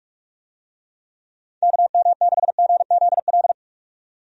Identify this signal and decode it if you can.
Morse code, heard as a keyed tone, about 46 wpm, 700 Hz. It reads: KM6GZL
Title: USA Call Signs - 6 Characters - 45wpm